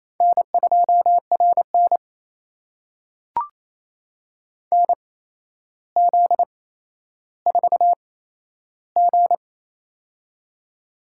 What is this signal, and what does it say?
Morse code, heard as a keyed tone, about 28 wpm, spaced as for 12 wpm, 700 Hz. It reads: N2RN NZ4G
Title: USA Call Signs - 4 Characters - 8x Character Spacing - 28wpm